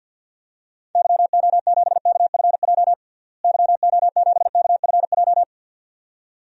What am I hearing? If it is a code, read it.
Morse code, heard as a keyed tone, about 53 wpm, 700 Hz. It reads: YO7KFJ YO7KFJ
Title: DXCC-100 Call Signs - 6 Characters - Repeated 2x - 50wpm